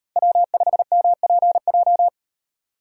Morse code, heard as a keyed tone, about 38 wpm, 700 Hz. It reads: W5MPJ